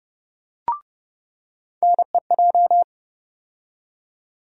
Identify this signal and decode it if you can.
Morse code, heard as a keyed tone, about 30 wpm, 700 Hz. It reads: NEJ
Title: USA Call Sign Postfixes - 3 Letters 30wpm